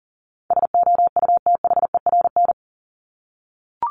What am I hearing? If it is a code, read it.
Morse code, heard as a keyed tone, about 40 wpm, 700 Hz. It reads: SOUTHERN
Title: Top 1751-2000 Words 40wpm